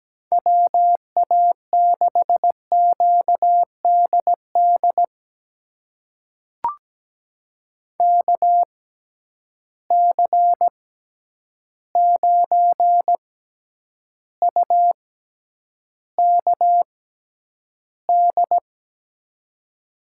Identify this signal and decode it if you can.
Morse code, heard as a keyed tone, about 17 wpm, spaced as for 9 wpm, 700 Hz. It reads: WA6QDD KC9UKD
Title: USA Call Signs - 6 Characters - 6x Character Spacing - 17wpm